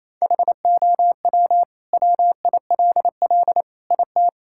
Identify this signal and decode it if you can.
Morse code, heard as a keyed tone, about 28 wpm, 700 Hz. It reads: HOW WILL IT
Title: Sentences from Top 1000 Words - Review Part 6 - 28wpm